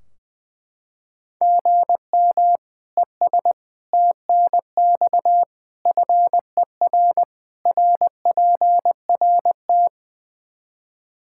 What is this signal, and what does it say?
Morse code, heard as a keyed tone, about 20 wpm, 700 Hz. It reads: GM ES TNX FER RPRT
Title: K1USN SST QSO Practice 20wpm